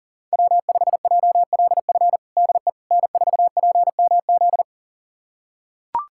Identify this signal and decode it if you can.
Morse code, heard as a keyed tone, about 40 wpm, 700 Hz. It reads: W5JLF DE N4PMZ
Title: USA Callsign DE Callsign - 5 Characters - 40wpm